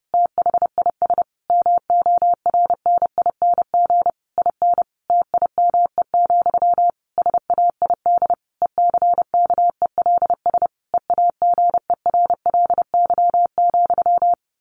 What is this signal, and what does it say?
Morse code, heard as a keyed tone, about 28 wpm, 700 Hz. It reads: THIS MORNING IN TIME, SAID ECKELS EAGERLY,
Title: A Sound Of Thunder 30wpm